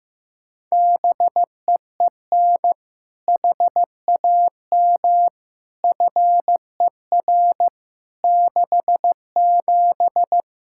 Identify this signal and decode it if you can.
Morse code, heard as a keyed tone, about 15 wpm, 700 Hz. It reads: BEEN HAM FER 67
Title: Practice QSO - Exchange 3 - More Info - 15wpm